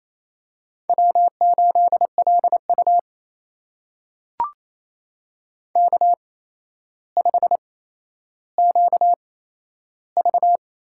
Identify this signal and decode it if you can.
Morse code, heard as a keyed tone, about 28 wpm, spaced as for 12 wpm, 700 Hz. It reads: W8LU K5QV
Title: USA Call Signs - 4 Characters - 8x Character Spacing - 28wpm